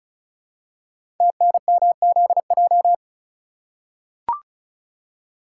Morse code, heard as a keyed tone, about 36 wpm, 700 Hz. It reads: TNMZJ
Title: Code Groups - 5 Letters - 35wpm